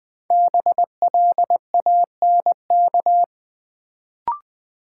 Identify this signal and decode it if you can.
Morse code, heard as a keyed tone, about 20 wpm, 700 Hz. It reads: BLANK
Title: Sentences from Top 1500 Words - New Word Drill - 20wpm